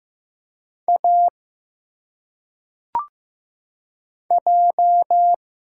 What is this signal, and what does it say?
Morse code, heard as a keyed tone, about 15 wpm, 700 Hz. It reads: A J
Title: Single Letters - Phonetic Pronunciation 15wpm